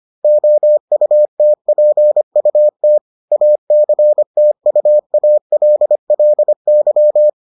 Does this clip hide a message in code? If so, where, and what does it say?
Yes: Morse code from 0.2 to 7.3 s, heard as a keyed tone, about 25 wpm, 600 Hz. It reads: OUTPUT ACTUALLY